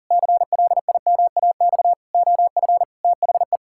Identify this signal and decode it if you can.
Morse code, heard as a keyed tone, about 40 wpm, 700 Hz. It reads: CLIMAX OF THE
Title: Pangram Challenge v2 - 40wpm